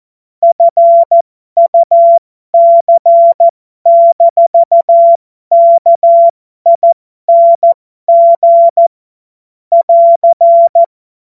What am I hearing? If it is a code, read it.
Morse code, heard as a keyed tone, about 14 wpm, spaced as for 12 wpm, 680 Hz. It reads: FUC-KING <AR>
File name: Morsekurs / Witze